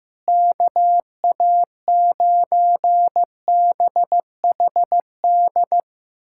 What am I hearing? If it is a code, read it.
Morse code, heard as a keyed tone, about 16 wpm, 700 Hz. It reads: KA9BHD
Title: K1USN SST QSO Practice - 15wpm